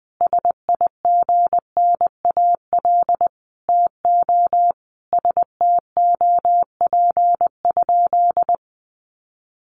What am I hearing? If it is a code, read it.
Morse code, heard as a keyed tone, about 20 wpm, 700 Hz. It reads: SIGNAL TO STOP?
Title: Sentences from Top 1400 Words 20wpm